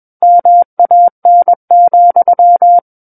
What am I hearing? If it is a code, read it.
Morse code, heard as a keyed tone, about 21 wpm, 700 Hz. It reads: MAN,
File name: Book One / 12-Chapter 12 - 21 WPM